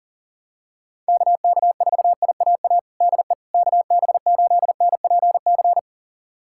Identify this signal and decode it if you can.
Morse code, heard as a keyed tone, about 39 wpm, 700 Hz. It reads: KK4IAA DE KB8NPC
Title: USA Callsign DE Callsign - 6 Characters - 40wpm